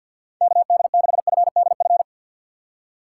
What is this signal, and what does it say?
Morse code, heard as a keyed tone, about 53 wpm, 700 Hz. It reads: KD6FDF
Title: USA Call Signs - 6 Characters - 3x Character Spacing - 50wpm